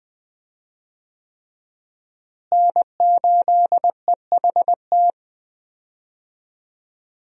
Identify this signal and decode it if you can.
Morse code, heard as a keyed tone, about 20 wpm, 700 Hz. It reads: N8EHT
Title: USA Call Signs - 5 Characters - Repeated 3x - 20wpm